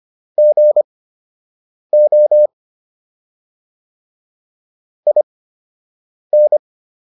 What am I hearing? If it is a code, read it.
Morse code, heard as a keyed tone, about 25 wpm, spaced as for 7 wpm, 600 Hz. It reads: GO IN